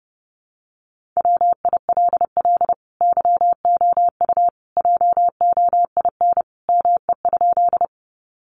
Morse code, heard as a keyed tone, about 30 wpm, 700 Hz. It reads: WILL YOU JOIN ME?
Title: Sentences from Top 1400 Words 30wpm